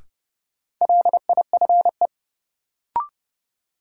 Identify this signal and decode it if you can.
Morse code, heard as a keyed tone, about 30 wpm, 700 Hz. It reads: LIFE
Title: Sentences from Top 300 Words - New Word Drill - 30wpm